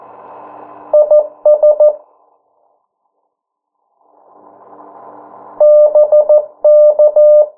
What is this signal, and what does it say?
Morse code, heard as a keyed tone, about 14 wpm, 600 Hz. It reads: IS BK
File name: morse mp3 / 09 urok 8